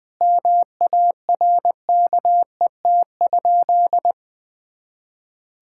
Morse code, heard as a keyed tone, about 20 wpm, 700 Hz. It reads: MARKET?
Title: Sentences from Top 1000 Words - Review Part 5 - 20wpm